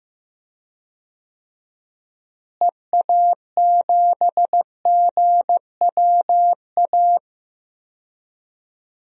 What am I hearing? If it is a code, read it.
Morse code, heard as a keyed tone, about 15 wpm, 700 Hz. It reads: EA7GWA
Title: DXCC-100 Call Signs - 6 Characters - Repeated 3x - 15wpm